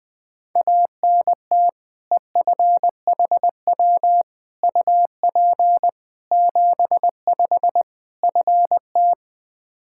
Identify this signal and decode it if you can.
Morse code, heard as a keyed tone, about 21 wpm, 700 Hz. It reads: ANT EFHW UP 75 FT
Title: Practice QSO - Exchange 2 - Info - 20wpm